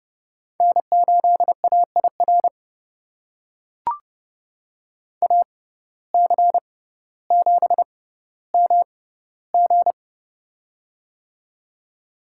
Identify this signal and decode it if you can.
Morse code, heard as a keyed tone, about 30 wpm, spaced as for 15 wpm, 700 Hz. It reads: N8AIR AC7MG
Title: USA Call Signs - 5 Characters - 6x Character Spacing - 30wpm